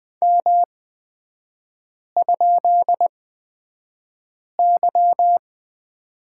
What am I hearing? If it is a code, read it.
Morse code, heard as a keyed tone, about 20 wpm, spaced as for 5 wpm, 700 Hz. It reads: M?Y
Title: 224 Lesson 36e - BT - Three Characters 20wpm